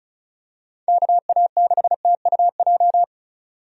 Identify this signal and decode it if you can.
Morse code, heard as a keyed tone, about 36 wpm, 700 Hz. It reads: KA6TUJ
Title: USA Call Signs - 6 Characters - 35wpm